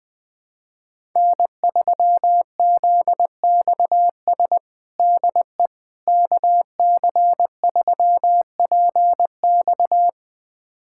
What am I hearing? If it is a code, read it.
Morse code, heard as a keyed tone, about 20 wpm, 700 Hz. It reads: N3ZXS DE KC3PX
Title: USA Callsign DE Callsign - 5 Characters - 20wpm